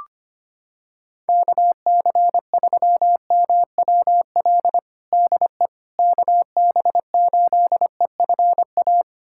Morse code, heard as a keyed tone, about 25 wpm, 700 Hz. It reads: KC3MWL DE KB8EFA
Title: USA Callsign DE Callsign - 6 Characters - 25wpm